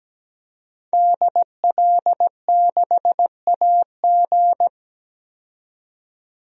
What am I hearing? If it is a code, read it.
Morse code, heard as a keyed tone, about 17 wpm, 700 Hz. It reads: DL6AG
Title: DXCC-100 Call Signs - 5 Characters - 17wpm